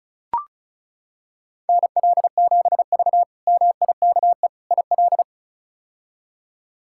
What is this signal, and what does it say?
Morse code, heard as a keyed tone, about 34 wpm, 700 Hz. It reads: NL7V MIKE IL
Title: K1USN SST QSO Practice - 35wpm